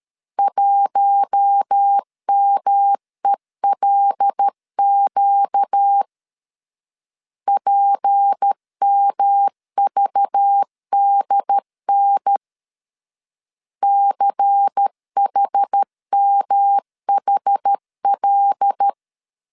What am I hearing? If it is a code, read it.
Morse code, heard as a keyed tone, about 13 wpm, 800 Hz. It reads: 1MELQ PMVDN CHMHL
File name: LEZIONI / 6 PKX0 / 6 P 12wpm 2033